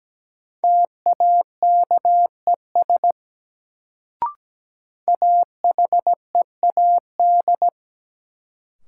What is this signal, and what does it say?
Morse code, heard as a keyed tone, about 17 wpm, 700 Hz. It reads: TAKES AHEAD
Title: Sentences from Top 1000 Words - Review Part 6 - New Word Drill - 17wpm